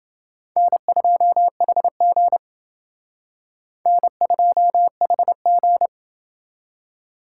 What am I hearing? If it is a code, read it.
Morse code, heard as a keyed tone, about 28 wpm, 700 Hz. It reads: N2HG N2HG